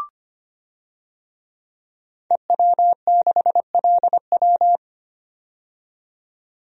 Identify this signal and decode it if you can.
Morse code, heard as a keyed tone, about 25 wpm, 700 Hz. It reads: EW6LW
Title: DXCC-100 Call Signs - 5 Characters - 25wpm